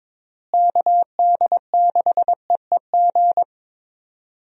Morse code, heard as a keyed tone, about 22 wpm, 700 Hz. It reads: KD6EEG
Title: USA Call Signs - 6 Characters - Repeated 2x - 22wpm